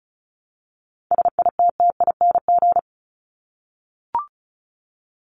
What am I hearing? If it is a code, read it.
Morse code, heard as a keyed tone, about 34 wpm, 700 Hz. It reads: SITTING
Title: ING Suffix - Top 100 Words with Base Words 35wpm